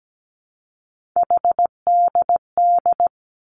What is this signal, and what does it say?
Morse code, heard as a keyed tone, about 17 wpm, 700 Hz. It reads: HDD
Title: Three Challenging Letter-Number Combinations 17wpm